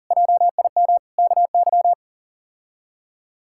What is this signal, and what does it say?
Morse code, heard as a keyed tone, about 40 wpm, 700 Hz. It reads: JIM KY
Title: K1USN SST QSO Practice - 40wpm